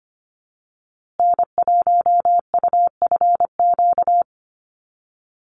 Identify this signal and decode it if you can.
Morse code, heard as a keyed tone, about 25 wpm, 700 Hz. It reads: N1UFQ
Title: USA Call Signs Repeated 3x 25wpm